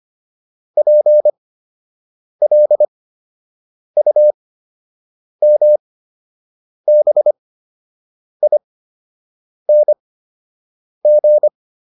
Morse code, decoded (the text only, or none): PLUMBING